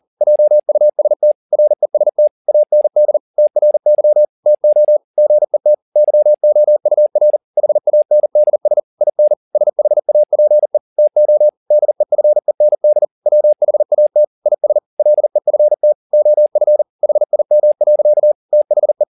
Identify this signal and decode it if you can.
Morse code, heard as a keyed tone, about 40 wpm, 600 Hz. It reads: JUST REST AND TRY TO GET YOUR HANDS IN SHAPE TO DEFEND WHAT IS LEFT OF HIM. THE
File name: the old man and the sea 2 40wpm